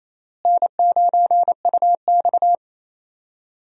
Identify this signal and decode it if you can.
Morse code, heard as a keyed tone, about 28 wpm, 700 Hz. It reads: N9UX
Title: USA Call Signs - 4 Characters - Repeated 3x - 28wpm